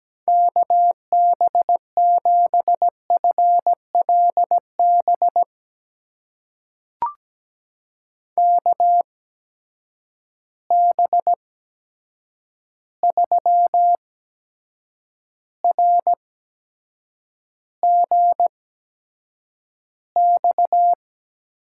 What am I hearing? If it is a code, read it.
Morse code, heard as a keyed tone, about 17 wpm, spaced as for 7 wpm, 700 Hz. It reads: KB7FLB KB3RGX
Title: USA Call Signs - 6 Characters - 8x Character Spacing - 17wpm